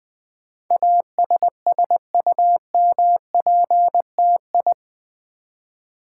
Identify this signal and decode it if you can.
Morse code, heard as a keyed tone, about 20 wpm, 700 Hz. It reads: ASSUMPTI
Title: Build A Word - 4 Letters At A Time - 10-Letter Words - 20wpm